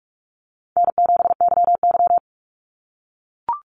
Morse code, heard as a keyed tone, about 44 wpm, 700 Hz. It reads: N7YY
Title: Easy Call Signs - 45wpm